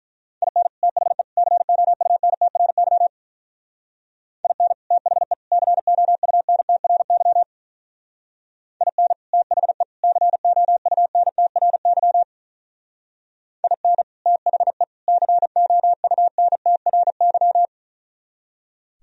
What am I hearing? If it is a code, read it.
Morse code, heard as a keyed tone, about 44 wpm, 700 Hz. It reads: IN THE COUNTRY IN THE COUNTRY IN THE COUNTRY IN THE COUNTRY
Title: Speed-Racing - Sets of 3 Words 35wpm